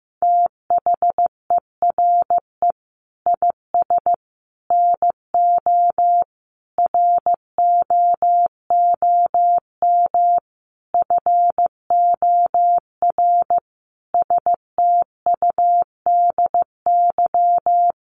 Morse code, heard as a keyed tone, about 15 wpm, 700 Hz. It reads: THERE IS NO ROOM FOR STUDY
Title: Sentences from Top 1200 Words 15wpm